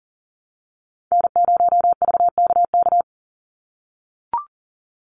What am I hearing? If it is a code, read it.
Morse code, heard as a keyed tone, about 40 wpm, 700 Hz. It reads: N0VKK
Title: USA Call Signs 40wpm